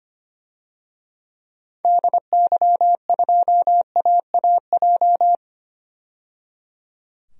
Morse code, heard as a keyed tone, about 25 wpm, 700 Hz. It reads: DY2AAJ
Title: DXCC-100 Call Signs - 6 Characters - 25wpm